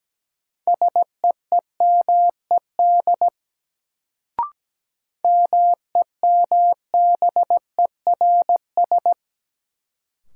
Morse code, heard as a keyed tone, about 17 wpm, 700 Hz. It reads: SEEMED MEMBERS
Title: Sets of 3 Words - New Word Drill - 17wpm